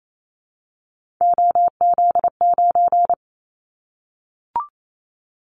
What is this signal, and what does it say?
Morse code, heard as a keyed tone, about 28 wpm, 700 Hz. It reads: OZ9
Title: Callsign Prefixes for Top 100 DXCC - 28wpm